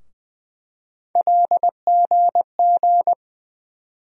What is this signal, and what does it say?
Morse code, heard as a keyed tone, about 20 wpm, 700 Hz. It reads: LGG
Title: USA Call Sign Postfixes - 3 Letters 20wpm